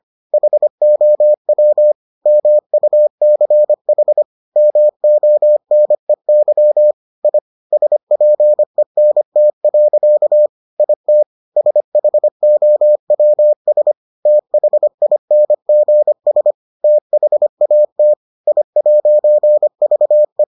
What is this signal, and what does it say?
Morse code, heard as a keyed tone, about 25 wpm, 600 Hz. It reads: HOW MUCH MONEY I SPENT. IT SHOWS THINGS THAT I'VE